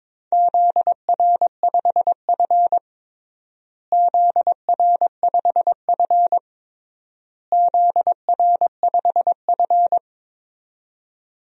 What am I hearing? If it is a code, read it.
Morse code, heard as a keyed tone, about 22 wpm, 700 Hz. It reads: ZR5F ZR5F ZR5F
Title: DXCC-100 Call Signs - 4 Characters - Repeated 3x - 22wpm